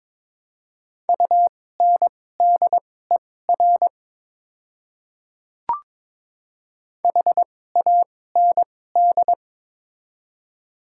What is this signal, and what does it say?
Morse code, heard as a keyed tone, about 22 wpm, spaced as for 16 wpm, 700 Hz. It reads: UNDER HAND
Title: 2x Character Spacing - Top 200 Words - 22wpm